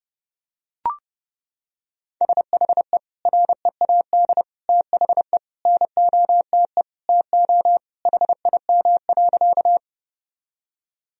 Morse code, heard as a keyed tone, about 30 wpm, 700 Hz. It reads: SHE READ THE NOTE TO HIM.
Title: Sentences from Top 1000 Words - Review Part 2 - 30wpm